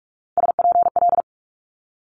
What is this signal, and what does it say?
Morse code, heard as a keyed tone, about 46 wpm, 700 Hz. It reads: SPL
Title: Multi-letter Phonemes Quick Fire 45wpm